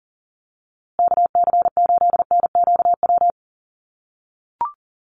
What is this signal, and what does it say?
Morse code, heard as a keyed tone, about 40 wpm, 700 Hz. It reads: KC8NQW
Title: USA Call Signs Repeated 3x 40wpm